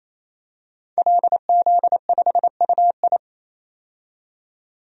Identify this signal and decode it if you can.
Morse code, heard as a keyed tone, about 28 wpm, 700 Hz. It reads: LZ5UI
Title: DXCC-100 Call Signs - 5 Characters - 28wpm